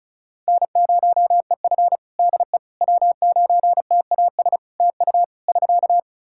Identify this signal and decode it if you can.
Morse code, heard as a keyed tone, about 34 wpm, 700 Hz. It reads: N0EF DE W9TAS TU <SK>
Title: Practice QSO - Exchange 4 - Ending - 35wpm